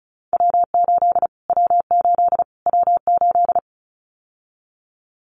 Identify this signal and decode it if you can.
Morse code, heard as a keyed tone, about 34 wpm, 700 Hz. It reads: W8 W8 W8
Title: Callsign Prefixes for Top 25 DXCC - Repeated 3x - 35wpm